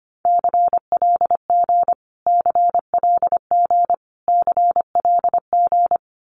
Morse code, heard as a keyed tone, about 25 wpm, 700 Hz. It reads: CLG CLG CLG
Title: Abbreviations Repeated 3x - 25wpm